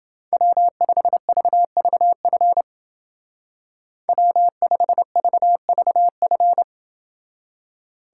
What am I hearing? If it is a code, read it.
Morse code, heard as a keyed tone, about 29 wpm, 700 Hz. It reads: W5VVF W5VVF